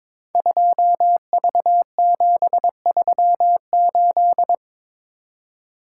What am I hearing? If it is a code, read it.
Morse code, heard as a keyed tone, about 22 wpm, 690 Hz. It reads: 2V738
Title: Code Groups - 5 Letter:Numbers - 22wpm